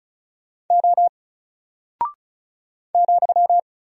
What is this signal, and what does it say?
Morse code, heard as a keyed tone, about 36 wpm, 700 Hz. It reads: O ,
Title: Single Character (Letter:Number:Punctuation:Prosign) Alt 35wpm